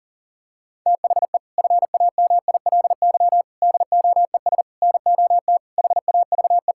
Morse code, heard as a keyed tone, about 40 wpm, 700 Hz. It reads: THE FAMILY DOES NOT HAVE
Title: Sentences from Top 1600 Words 40wpm